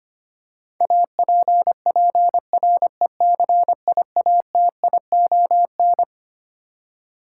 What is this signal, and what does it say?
Morse code, heard as a keyed tone, about 25 wpm, 700 Hz. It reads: APPRECIATION
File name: Twelve Letter Words - 25wpm